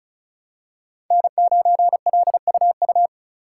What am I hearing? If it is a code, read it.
Morse code, heard as a keyed tone, about 36 wpm, 700 Hz. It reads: N9LUU
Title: USA Call Signs - 5 Characters - 4x Character Spacing - 35wpm